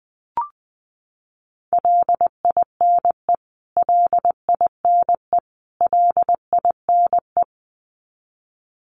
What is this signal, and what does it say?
Morse code, heard as a keyed tone, about 20 wpm, 700 Hz. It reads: LINE LINE LINE
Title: Top 200 Words Repeated 20wpm